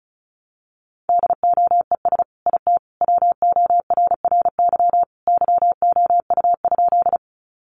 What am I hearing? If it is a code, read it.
Morse code, heard as a keyed tone, about 34 wpm, 700 Hz. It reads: DOES IT WORRY YOU?
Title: Sentences from Top 1100 Words 35wpm